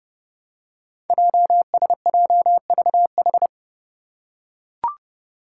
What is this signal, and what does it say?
Morse code, heard as a keyed tone, about 30 wpm, 700 Hz. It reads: JSJVH
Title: Five Challenging Letter-Number Combinations 30wpm